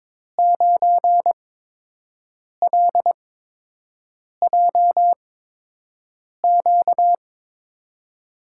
Morse code, heard as a keyed tone, about 22 wpm, spaced as for 6 wpm, 700 Hz. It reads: 9LJQ
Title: USA Call Signs - 5 Characters - 8x Character Spacing - 22wpm